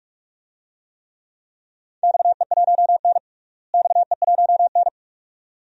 Morse code, heard as a keyed tone, about 46 wpm, 700 Hz. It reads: XE1N XE1N
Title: DXCC-100 Call Signs - 4 Characters - Repeated 2x - 45wpm